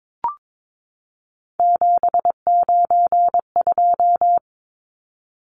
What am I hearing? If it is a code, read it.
Morse code, heard as a keyed tone, about 22 wpm, 700 Hz. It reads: 792
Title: Numbers - 3-Digit 22wpm